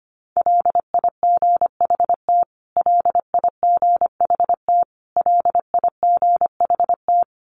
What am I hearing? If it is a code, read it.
Morse code, heard as a keyed tone, about 25 wpm, 700 Hz. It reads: LIGHT LIGHT LIGHT
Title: Top 300 Words Repeated 25wpm